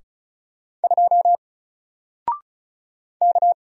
Morse code, heard as a keyed tone, about 36 wpm, 700 Hz. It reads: 2 K
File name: Single-Letter-Number-35wpm